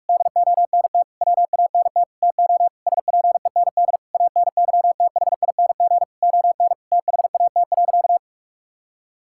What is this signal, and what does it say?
Morse code, heard as a keyed tone, about 46 wpm, 700 Hz. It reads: DONT WANT TO SPEND ANYTHING ON THAT.
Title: Sentences from Top 1000 Words - Review Part 5 - 45wpm